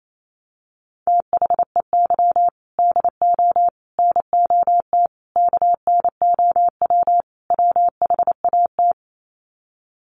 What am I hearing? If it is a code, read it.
Morse code, heard as a keyed tone, about 28 wpm, 700 Hz. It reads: THEY DO NOT KNOW WHAT
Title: Sets of 5 Words Encore 28wpm